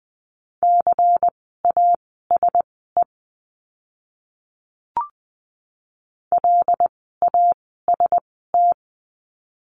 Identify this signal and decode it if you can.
Morse code, heard as a keyed tone, about 20 wpm, spaced as for 14 wpm, 700 Hz. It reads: CASE LAST
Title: 244 Lesson 39f - Letter z - Words 20wpm